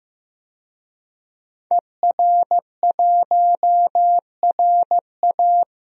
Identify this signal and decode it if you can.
Morse code, heard as a keyed tone, about 16 wpm, 700 Hz. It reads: ER1RA
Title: DXCC-100 Call Signs - 5 Characters - 15wpm